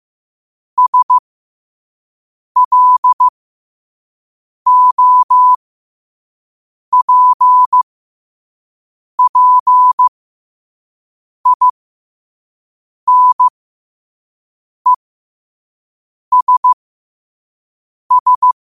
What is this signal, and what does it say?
Morse code, heard as a keyed tone, about 15 wpm, spaced as for 5 wpm, 1000 Hz. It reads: SLOPPINESS
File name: qotdfiles / 5wpm